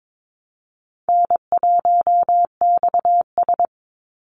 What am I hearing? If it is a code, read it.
Morse code, heard as a keyed tone, about 22 wpm, 700 Hz. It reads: N1XS
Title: Easy Call Signs Repeated 2x - 22wpm